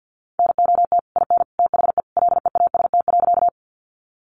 Morse code, heard as a keyed tone, about 50 wpm, 700 Hz. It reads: NOT IN THE LEAST.
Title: Sentences from Top 700 Words 50wpm